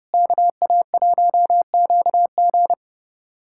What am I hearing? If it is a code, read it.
Morse code, heard as a keyed tone, about 30 wpm, 700 Hz. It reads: KA1QG
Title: USA Callsign DE Callsign - 5 Characters - 30wpm